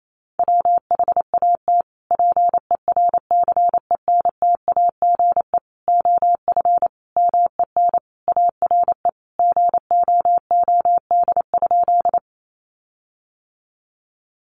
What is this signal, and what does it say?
Morse code, heard as a keyed tone, about 28 wpm, 700 Hz. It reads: WHAT PERCENTAGE OF MEN ARE GOOD?
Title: Sentences from Top 1300 Words 28wpm